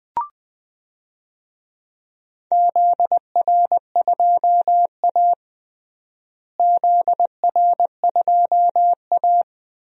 Morse code, heard as a keyed tone, about 20 wpm, 690 Hz. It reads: ZR2A ZR2A
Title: DXCC-100 Call Signs - 4 Characters - Repeated 3x - 20wpm